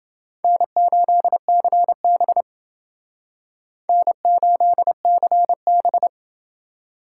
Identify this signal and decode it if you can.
Morse code, heard as a keyed tone, about 28 wpm, 700 Hz. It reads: N8CB N8CB